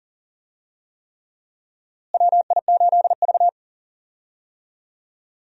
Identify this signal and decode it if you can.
Morse code, heard as a keyed tone, about 39 wpm, 700 Hz. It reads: WI8V
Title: USA Call Signs - 4 Characters - Repeated 3x - 40wpm